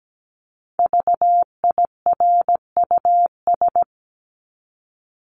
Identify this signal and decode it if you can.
Morse code, heard as a keyed tone, about 17 wpm, 700 Hz. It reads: VIRUS
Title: Top 1251-1500 Words 17wpm